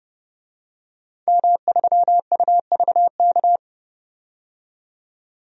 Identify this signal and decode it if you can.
Morse code, heard as a keyed tone, about 30 wpm, 700 Hz. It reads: M3UVK
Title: DXCC-100 Call Signs - 5 Characters - 30wpm